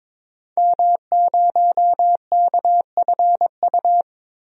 Morse code, heard as a keyed tone, about 21 wpm, 700 Hz. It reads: M0KFU
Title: DXCC-100 Call Signs - 5 Characters - Repeated 2x - 22wpm